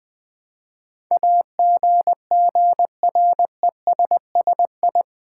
Morse code, heard as a keyed tone, about 20 wpm, 700 Hz. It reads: AGGRESSI
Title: Build A Word - 4 Letters At A Time - 10-Letter Words - 20wpm